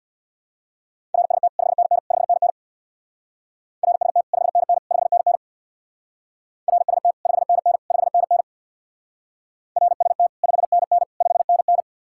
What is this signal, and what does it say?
Morse code, heard as a keyed tone, about 60 wpm, 700 Hz. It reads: RST 5NN 5NN RST 5NN 5NN RST 5NN 5NN RST 5NN 5NN
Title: Speed-Racing Common QSO Phrases Encore - 50wpm